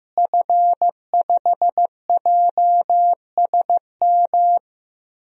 Morse code, heard as a keyed tone, about 15 wpm, 700 Hz. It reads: F5JSM
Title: DXCC-100 Call Signs - 5 Characters - Repeated 3x - 15wpm